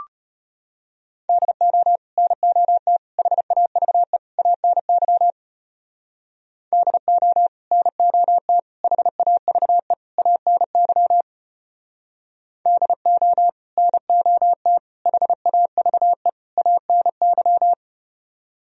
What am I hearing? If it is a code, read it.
Morse code, heard as a keyed tone, about 33 wpm, 700 Hz. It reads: DO NOT HAVE ANY DO NOT HAVE ANY DO NOT HAVE ANY